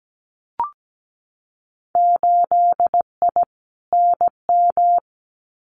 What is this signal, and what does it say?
Morse code, heard as a keyed tone, about 17 wpm, 700 Hz. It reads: 8I NM
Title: Winter Field Day Exchanges Encore 17wpm